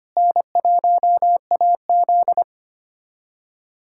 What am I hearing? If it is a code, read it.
Morse code, heard as a keyed tone, about 25 wpm, 700 Hz. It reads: N1AZ
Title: USA Callsign DE Callsign - 4 Characters - 25wpm